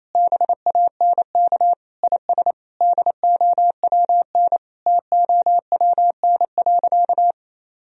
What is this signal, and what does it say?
Morse code, heard as a keyed tone, about 28 wpm, 700 Hz. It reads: BANK IS DOWN TOWN.
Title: Sentences from Top 1000 Words - Review Part 4 - 28wpm